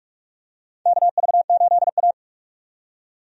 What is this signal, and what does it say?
Morse code, heard as a keyed tone, about 46 wpm, 700 Hz. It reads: KV8A
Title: USA Call Signs - 4 Characters - 45wpm